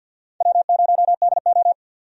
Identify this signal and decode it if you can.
Morse code, heard as a keyed tone, about 49 wpm, 700 Hz. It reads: W0DO
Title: USA Call Signs - 4 Characters - 50wpm